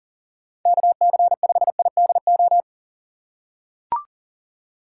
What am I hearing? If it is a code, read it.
Morse code, heard as a keyed tone, about 40 wpm, 690 Hz. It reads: KC5IDO
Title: USA Call Signs - 2x Character Spacing - 40wpm